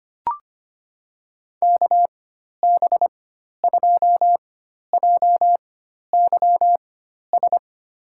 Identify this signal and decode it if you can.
Morse code, heard as a keyed tone, about 25 wpm, spaced as for 12 wpm, 700 Hz. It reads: KB2JYS
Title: USA Call Signs - 6 Characters - 4x Character Spacing - 25wpm